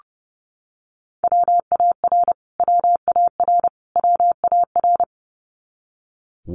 Morse code, heard as a keyed tone, about 30 wpm, 700 Hz. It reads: WAR WAR WAR
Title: Top 500 Words Repeated 30wpm